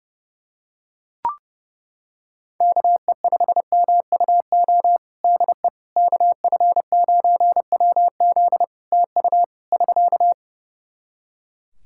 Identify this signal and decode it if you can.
Morse code, heard as a keyed tone, about 32 wpm, 700 Hz. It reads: KE5MUO DE KF9WZ TU <SK>
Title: K1USN SST QSO Practice 30wpm